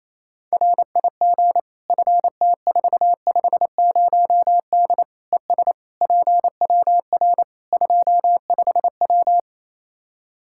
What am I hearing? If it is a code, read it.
Morse code, heard as a keyed tone, about 28 wpm, 700 Hz. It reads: RIG FT450D ES PWR 25W
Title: K1USN SST QSO Practice 28wpm